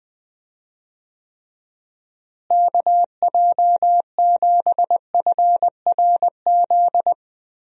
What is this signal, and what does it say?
Morse code, heard as a keyed tone, about 20 wpm, 700 Hz. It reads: KJ7FRZ
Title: USA Call Signs - 6 Characters - Repeated 3x - 20wpm